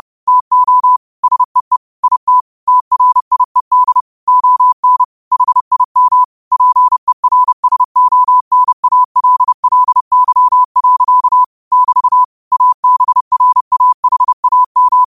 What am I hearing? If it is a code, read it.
Morse code, heard as a keyed tone, about 30 wpm, 1000 Hz. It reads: TO SEE IT TRIED ON HIM PERSONALLY. <BT> ABRAHAM
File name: qotdfiles / 30wpm